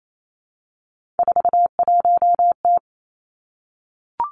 Morse code, heard as a keyed tone, about 28 wpm, 700 Hz. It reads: 41T
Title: Three Challenging Letter-Number Combinations 28wpm